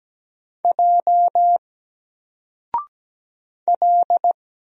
Morse code, heard as a keyed tone, about 17 wpm, 700 Hz. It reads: J L
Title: Single Character (Letter:Number:Punctuation:Prosign) 17wpm